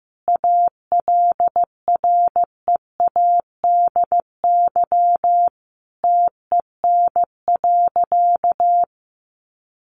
Morse code, heard as a keyed tone, about 15 wpm, 700 Hz. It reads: ALREADY TEN.
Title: Sentences from Top 700 Words 15wpm